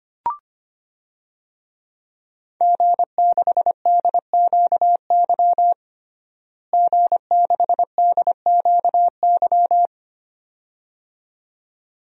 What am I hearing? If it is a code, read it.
Morse code, heard as a keyed tone, about 25 wpm, 700 Hz. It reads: G6DQY G6DQY
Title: DXCC-100 Call Signs - 5 Characters - Repeated 2x - 25wpm